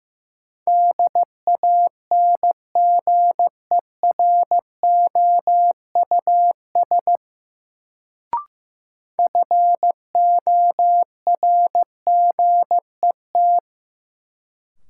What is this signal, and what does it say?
Morse code, heard as a keyed tone, about 15 wpm, 700 Hz. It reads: DANGEROUS FORGET
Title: Sentences from Top 1200 Words - New Word Drill Alt - 15wpm